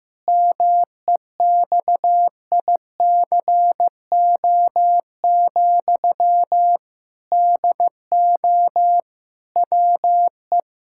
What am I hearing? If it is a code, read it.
Morse code, heard as a keyed tone, about 15 wpm, 700 Hz. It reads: MEXICO, DO WE
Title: Pangram Challenge v2 - 15wpm